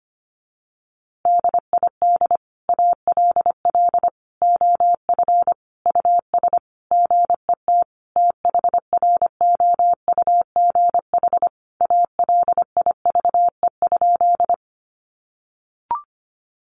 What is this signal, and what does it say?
Morse code, heard as a keyed tone, about 25 wpm, 700 Hz. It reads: DID ALL OF US GET THROUGH ALIVE?
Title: A Sound Of Thunder 25wpm